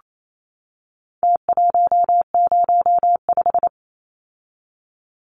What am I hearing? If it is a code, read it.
Morse code, heard as a keyed tone, about 28 wpm, 700 Hz. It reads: T105
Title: Numbers - Contest Style - 4-Digit 28wpm